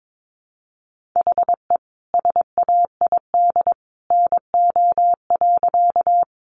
Morse code, heard as a keyed tone, about 22 wpm, 700 Hz. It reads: HE SAID NO.
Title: Sentences from Top 100 Words 22wpm